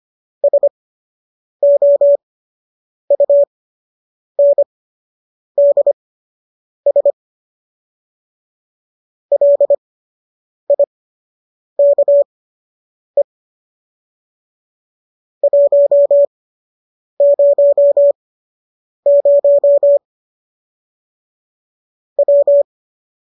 Morse code, heard as a keyed tone, about 25 wpm, spaced as for 8 wpm, 570 Hz. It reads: SOUNDS LIKE 100 W